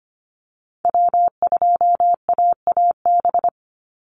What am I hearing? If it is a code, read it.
Morse code, heard as a keyed tone, about 25 wpm, 700 Hz. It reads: W2AAB
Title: Call Signs Repeated 2x - 25wpm